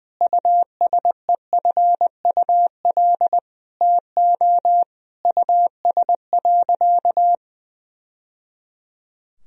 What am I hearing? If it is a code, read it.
Morse code, heard as a keyed tone, about 20 wpm, 700 Hz. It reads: USEFUL TO US.
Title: Sentences from Top 1000 Words - Review Part 4 - 20wpm